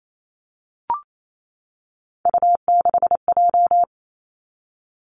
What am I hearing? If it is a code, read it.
Morse code, heard as a keyed tone, about 28 wpm, 700 Hz. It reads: U6J
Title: Three Challenging Letter-Number Combinations 28wpm